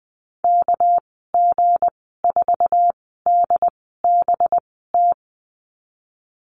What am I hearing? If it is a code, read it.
Morse code, heard as a keyed tone, about 20 wpm, spaced as for 15 wpm, 700 Hz. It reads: KG4DBT
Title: 207 Lesson 33g - Letter j - Call Signs 20wpm